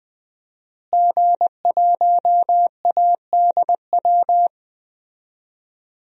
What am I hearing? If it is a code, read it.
Morse code, heard as a keyed tone, about 20 wpm, 700 Hz. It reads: G1ADW
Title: DXCC-100 Call Signs - 5 Characters - Repeated 2x - 20wpm